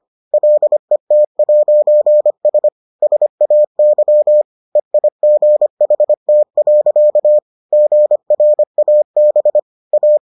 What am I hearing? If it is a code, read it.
Morse code, heard as a keyed tone, about 25 wpm, 600 Hz. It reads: LET'S SAY EIGHT. GRAB A